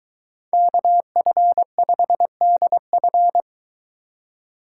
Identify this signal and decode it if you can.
Morse code, heard as a keyed tone, about 23 wpm, 700 Hz. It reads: KF5DF